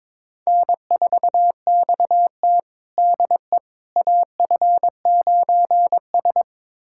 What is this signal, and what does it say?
Morse code, heard as a keyed tone, about 22 wpm, 700 Hz. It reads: N4XT DE AF9S
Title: USA Callsign DE Callsign - 4 Characters - 22wpm